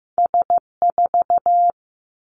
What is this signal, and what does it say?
Morse code, heard as a keyed tone, about 15 wpm, 700 Hz. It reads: S 4
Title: Two Challenging Letter-Number Combinations 15wpm